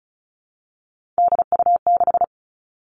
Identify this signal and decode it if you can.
Morse code, heard as a keyed tone, about 36 wpm, 700 Hz. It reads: DU6
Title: Three Challenging Letter-Number Combinations 35wpm